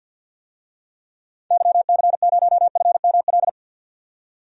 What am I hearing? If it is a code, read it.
Morse code, heard as a keyed tone, about 53 wpm, 700 Hz. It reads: YC0FML
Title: DXCC-100 Call Signs - 6 Characters - 50wpm